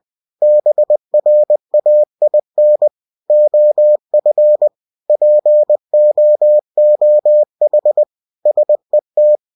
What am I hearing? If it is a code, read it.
Morse code, heard as a keyed tone, about 20 wpm, 600 Hz. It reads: BRAIN OF POOH SET
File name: cw-books / winnie-the-pooh / winnie-20wpm-page-0100